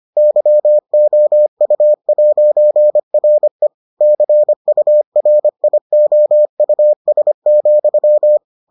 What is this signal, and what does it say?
Morse code, heard as a keyed tone, about 25 wpm, 600 Hz. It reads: YOU'RE CURIOUS,